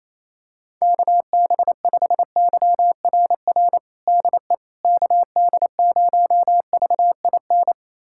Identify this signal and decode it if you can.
Morse code, heard as a keyed tone, about 28 wpm, 700 Hz. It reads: KB5YRR DE KD0VIN
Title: USA Callsign DE Callsign 28wpm